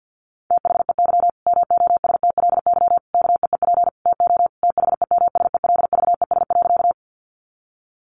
Morse code, heard as a keyed tone, about 50 wpm, 700 Hz. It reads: THEY MOSTLY KEEP TO THEMSELVES.
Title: Sentences from Top 1300 Words 50wpm